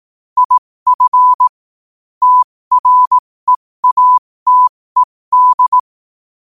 Morse code, heard as a keyed tone, about 18 wpm, spaced as for 15 wpm, 1000 Hz. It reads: IF TREATED
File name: qotdfiles / 15wpm